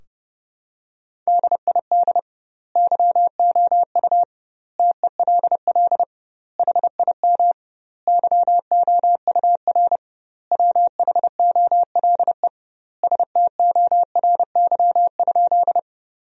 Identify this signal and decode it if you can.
Morse code, heard as a keyed tone, about 30 wpm, 700 Hz. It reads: DID YOU TELL HIM YOUR WHOLE STORY?
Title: 2x Word-Spacing - Sentences from Top 400 Words - 30wpm